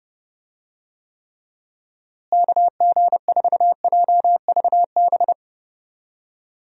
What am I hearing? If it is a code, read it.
Morse code, heard as a keyed tone, about 30 wpm, 700 Hz. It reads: KG4JVB
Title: USA Call Signs - 6 Characters - Repeated 2x - 30wpm